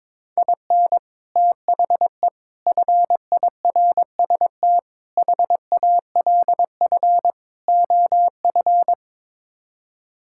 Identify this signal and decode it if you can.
Morse code, heard as a keyed tone, about 22 wpm, 700 Hz. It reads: IN THE FIRST HALF OF
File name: practice / 22wpm / Build A Phrase - 5-Word Phrases - 22wpm